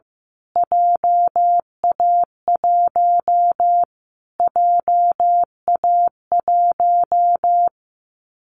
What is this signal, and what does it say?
Morse code, heard as a keyed tone, about 15 wpm, 700 Hz. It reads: JA1 JA1
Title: Callsign Prefixes for Top 25 DXCC - Repeated 3x - 15wpm